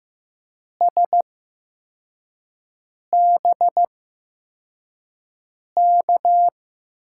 Morse code, heard as a keyed tone, about 15 wpm, spaced as for 4 wpm, 700 Hz. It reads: SBK
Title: USA Call Signs - 6 Characters - 8x Character Spacing - 15wpm